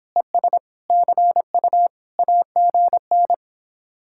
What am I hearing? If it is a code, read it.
Morse code, heard as a keyed tone, about 26 wpm, 700 Hz. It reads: ES CU AGN